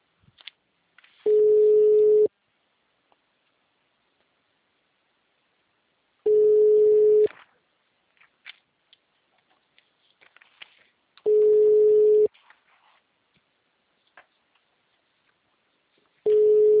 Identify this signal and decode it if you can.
Ringback tone (Europe, 425 Hz).